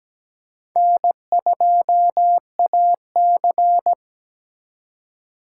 Morse code, heard as a keyed tone, about 20 wpm, 700 Hz. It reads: N2AC
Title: Speed Racing - USA 1x2 Call Signs 17wpm